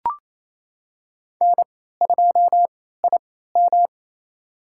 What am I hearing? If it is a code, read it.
Morse code, heard as a keyed tone, about 28 wpm, spaced as for 16 wpm, 700 Hz. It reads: N2IM
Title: USA Call Signs - 4 Characters - 3x Character Spacing - 28wpm